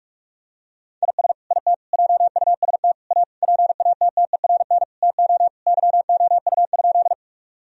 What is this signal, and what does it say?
Morse code, heard as a keyed tone, about 46 wpm, 700 Hz. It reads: IS IT JUST A PATTERN TO YOU?
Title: Sentences from Top 1000 Words - Review Part 2 - 45wpm